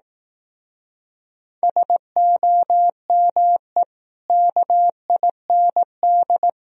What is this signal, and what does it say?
Morse code, heard as a keyed tone, about 19 wpm, 700 Hz. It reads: SOME KIND
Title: Speed-Racing - Sets of 3 Words 15wpm